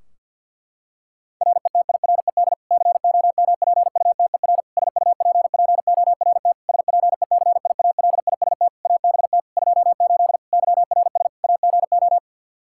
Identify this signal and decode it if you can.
Morse code, heard as a keyed tone, about 51 wpm, 700 Hz. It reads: RETIRED COMPUTER SUPPORT SPECIALIST ABT 28 YRS AGO
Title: Practice QSO - Exchange 3 - More Info - 50wpm